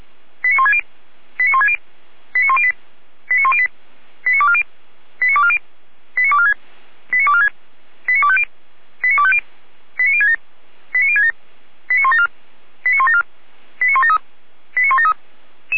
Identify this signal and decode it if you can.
Five-tone selective call (ZVEI), heard as quick runs of five tones, tones of about 70 ms. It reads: 89160 89160 89108 89108 89240 89240 89246 89246 89250 89250 89067 89067 89174 89174 89173 89173